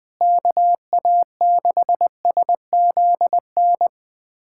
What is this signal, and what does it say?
Morse code, heard as a keyed tone, about 21 wpm, 700 Hz. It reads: KA6SZN
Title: USA Callsign DE Callsign - 6 Characters - 20wpm